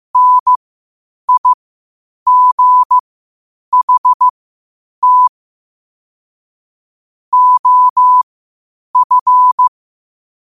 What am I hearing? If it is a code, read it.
Morse code, heard as a keyed tone, about 15 wpm, spaced as for 8 wpm, 1000 Hz. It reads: NIGHT OF
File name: qotdfiles / 8wpm